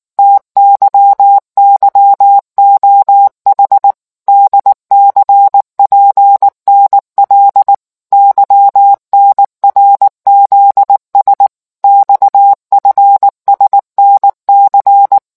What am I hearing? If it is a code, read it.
Morse code, heard as a keyed tone, about 19 wpm, 800 Hz. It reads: TYYOH DCPNL YNRZS XFSNC